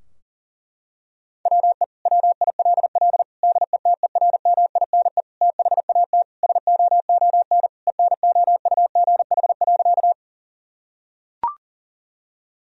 Morse code, heard as a keyed tone, about 40 wpm, 700 Hz. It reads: WE WILL DETERMINE THAT SOON ENOUGH.
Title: Sentences from Top 1000 Words - Review Part 4 - 40wpm